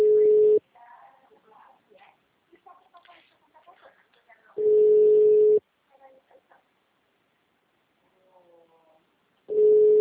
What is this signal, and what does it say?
Ringback tone (Europe, 425 Hz), with speech.
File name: out- -2011- - - .14757